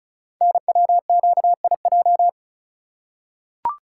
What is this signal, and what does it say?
Morse code, heard as a keyed tone, about 36 wpm, 700 Hz. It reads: NWQIJ
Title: Code Groups - 5 Letter:Numbers - 35wpm